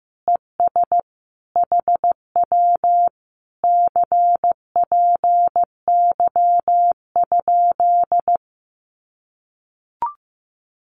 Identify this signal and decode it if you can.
Morse code, heard as a keyed tone, about 16 wpm, 700 Hz. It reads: ES HW CPY?
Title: Common QSO Phrases Encore 15wpm